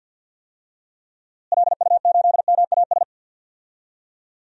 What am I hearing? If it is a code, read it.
Morse code, heard as a keyed tone, about 46 wpm, 700 Hz. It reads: LU8MAS
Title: DXCC-100 Call Signs - 6 Characters - Repeated 2x - 50wpm